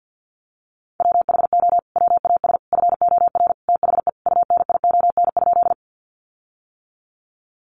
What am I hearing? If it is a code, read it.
Morse code, heard as a keyed tone, about 50 wpm, 700 Hz. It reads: WHO WAS FOR THE UNION?
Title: Sentences from Top 1400 Words 50wpm